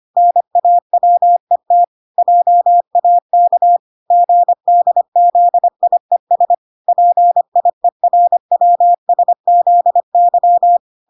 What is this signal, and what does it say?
Morse code, heard as a keyed tone, about 25 wpm, 700 Hz. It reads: NAWET JAK GDZIES PIERWSZY